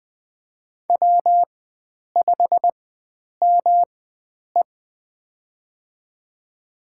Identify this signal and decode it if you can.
Morse code, heard as a keyed tone, about 20 wpm, spaced as for 9 wpm, 700 Hz. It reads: W5ME
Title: USA Call Signs - 4 Characters - 4x Character Spacing - 20wpm